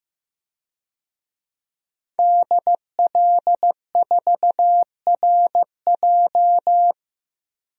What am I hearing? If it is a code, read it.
Morse code, heard as a keyed tone, about 15 wpm, 700 Hz. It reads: DL4RJ
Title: DXCC-100 Call Signs - 5 Characters - 15wpm